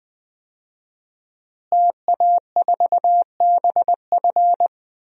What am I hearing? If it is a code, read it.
Morse code, heard as a keyed tone, about 20 wpm, 700 Hz. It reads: TA4BF
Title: DXCC-100 Call Signs - 5 Characters - Repeated 2x - 20wpm